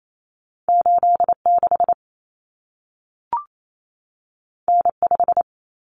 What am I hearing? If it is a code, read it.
Morse code, heard as a keyed tone, about 28 wpm, 700 Hz. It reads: 86 N5
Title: Common Cut Numbers - 2-Digit 28wpm